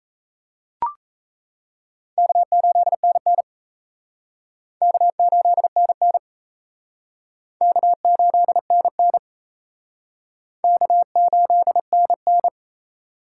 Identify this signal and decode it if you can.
Morse code, heard as a keyed tone, about 34 wpm, 700 Hz. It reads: K8NN K8NN K8NN K8NN
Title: Speed Racing - USA 1x2 Call Signs 28wpm